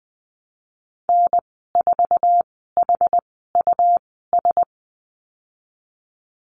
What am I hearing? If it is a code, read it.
Morse code, heard as a keyed tone, about 20 wpm, spaced as for 15 wpm, 700 Hz. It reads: N4HUS
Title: 122 Lesson 20g - Number 3 - Call Signs 20wpm